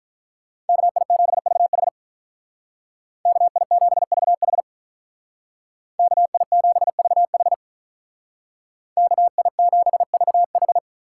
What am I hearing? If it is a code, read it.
Morse code, heard as a keyed tone, about 46 wpm, 700 Hz. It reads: KI7VH KI7VH KI7VH KI7VH